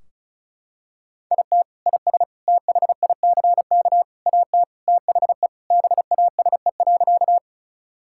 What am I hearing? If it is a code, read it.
Morse code, heard as a keyed tone, about 36 wpm, 700 Hz. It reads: IT IS THICK AT THE BASE.
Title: Sentences from Top 1600 Words 35wpm